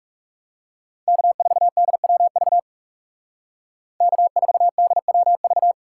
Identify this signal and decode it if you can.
Morse code, heard as a keyed tone, about 37 wpm, 700 Hz. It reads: K4DWV K4DWV
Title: Speed Racing - USA 1x3 Call Signs 30wpm